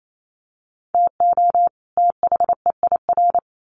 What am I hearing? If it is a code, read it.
Morse code, heard as a keyed tone, about 28 wpm, 700 Hz. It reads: TO THEIR
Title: Sets of 2 Words 28wpm